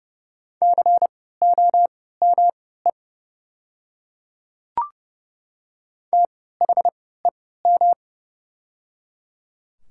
Morse code, heard as a keyed tone, about 30 wpm, spaced as for 17 wpm, 700 Hz. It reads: COME THEM
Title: 3x Character Spacing - Top 100 Words - 30wpm